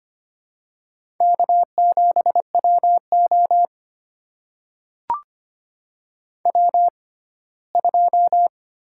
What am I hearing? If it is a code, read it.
Morse code, heard as a keyed tone, about 25 wpm, spaced as for 17 wpm, 700 Hz. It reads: K7WO W2
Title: USA Call Signs - 6x Character Spacing - 25wpm